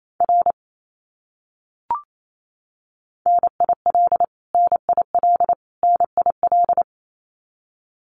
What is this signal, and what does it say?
Morse code, heard as a keyed tone, about 28 wpm, 700 Hz. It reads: R NIL NIL NIL
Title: Abbreviations Repeated 3x - 28wpm